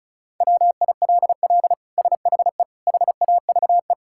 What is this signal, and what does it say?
Morse code, heard as a keyed tone, about 36 wpm, 700 Hz. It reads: WILL SHE HAVE
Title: Sentences from Top 1000 Words - Review Part 6 - 35wpm